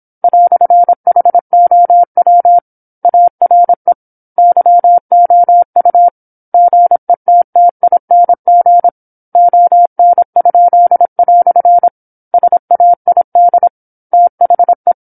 Morse code, heard as a keyed tone, about 27 wpm, 700 Hz. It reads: "HOW ARE YOU GETTING ON?" SAID THE
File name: 08-Chapter-08 26 WPM